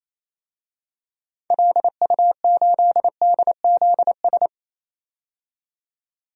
Morse code, heard as a keyed tone, about 28 wpm, 700 Hz. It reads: LU8DZS
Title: DXCC-100 Call Signs - 6 Characters - 28wpm